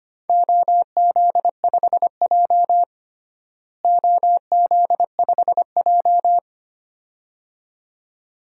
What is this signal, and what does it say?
Morse code, heard as a keyed tone, about 25 wpm, 700 Hz. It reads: OZ5J OZ5J
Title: DXCC-100 Call Signs - 4 Characters - Repeated 2x - 25wpm